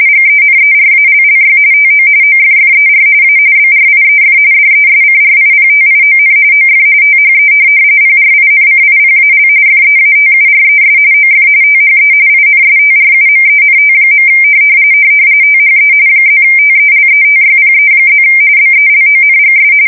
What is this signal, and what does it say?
RTTY, about 45.45 baud. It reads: GENERALLY TO OFFER A BRIEF DESCRIPTION OF THE PRINCIPAL FEATURES OF THE INSTRUMENT. THE +TELETYPE+ BELONGS
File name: fsk170 / morkrum